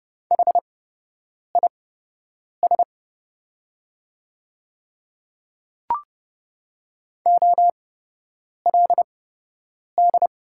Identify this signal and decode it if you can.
Morse code, heard as a keyed tone, about 30 wpm, spaced as for 8 wpm, 700 Hz. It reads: HIS OLD
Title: 8x Character Spacing - Top 200 Words - 30wpm